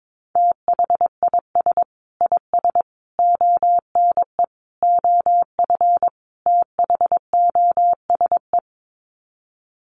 Morse code, heard as a keyed tone, about 22 wpm, 700 Hz. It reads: THIS IS ONE OF THOSE
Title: Sets of 5 Words Encore2 22wpm